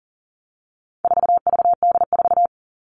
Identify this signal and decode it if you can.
Morse code, heard as a keyed tone, about 40 wpm, 700 Hz. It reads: 4VD4
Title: All Cut Numbers - 4-Digit 40wpm